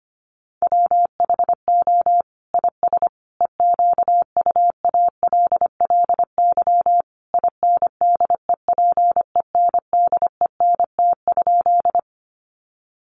Morse code, heard as a keyed tone, about 25 wpm, 700 Hz. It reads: WHO IS EQUALLY INDEPENDENT?
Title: Sentences from Top 1400 Words 25wpm